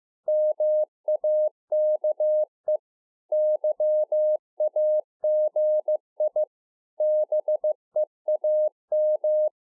Morse code, heard as a keyed tone, about 15 wpm, 610 Hz. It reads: MAKE YAGI BEAM